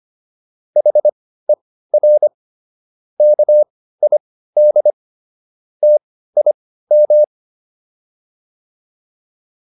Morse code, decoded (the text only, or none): HER KID TIM